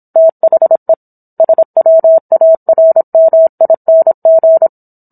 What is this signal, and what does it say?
Morse code, heard as a keyed tone, about 27 wpm, 650 Hz. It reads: THE SWARMING